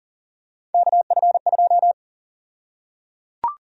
Morse code, heard as a keyed tone, about 38 wpm, 700 Hz. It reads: KF2
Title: USA Call Sign Prefixes - Repeated 2x - 40wpm